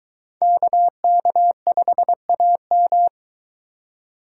Morse code, heard as a keyed tone, about 23 wpm, 700 Hz. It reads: KK5AM